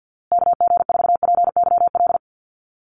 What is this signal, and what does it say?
Morse code, heard as a keyed tone, about 49 wpm, 700 Hz. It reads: KG4PYL
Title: USA Call Signs Repeated 2x 50wpm